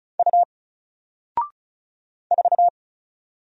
Morse code, heard as a keyed tone, about 36 wpm, 700 Hz. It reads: U 4
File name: Single-Letter-Number-35wpm